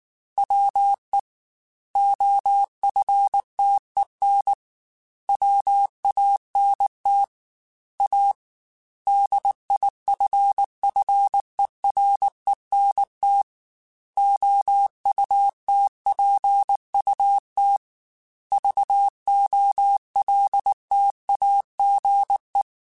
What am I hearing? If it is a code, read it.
Morse code, heard as a keyed tone, about 19 wpm, 800 Hz. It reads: WE OFTEN WANT A DIFFERENT OUTPUT VOLTAGE
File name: Technical Subjects / Technical Subjects-mp3s / Sw Pwr Sup @18